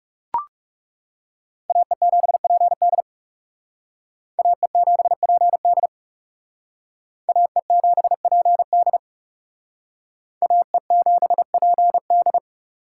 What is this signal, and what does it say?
Morse code, heard as a keyed tone, about 38 wpm, 700 Hz. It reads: AE7PD AE7PD AE7PD AE7PD